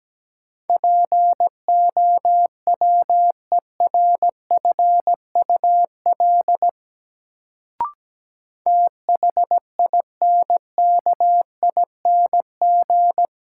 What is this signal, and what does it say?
Morse code, heard as a keyed tone, about 17 wpm, 700 Hz. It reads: POWERFUL THINKING
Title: Sentences from Top 1000 Words - Review Part 6 - New Word Drill - 17wpm